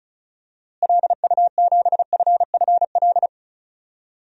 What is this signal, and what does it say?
Morse code, heard as a keyed tone, about 34 wpm, 700 Hz. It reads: LU7FFL
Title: DXCC-100 Call Signs - 6 Characters - Repeated 3x - 35wpm